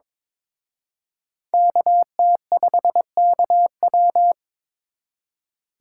Morse code, heard as a keyed tone, about 26 wpm, 700 Hz. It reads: KT5KW
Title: Speed Racing - USA Call Signs 22wpm